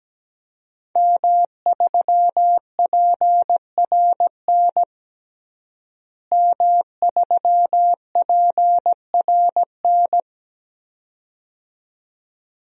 Morse code, heard as a keyed tone, about 17 wpm, 700 Hz. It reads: M3PRN M3PRN
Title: DXCC-100 Call Signs - 5 Characters - Repeated 2x - 17wpm